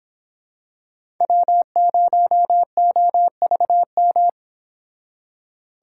Morse code, heard as a keyed tone, about 26 wpm, 700 Hz. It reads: W0OVM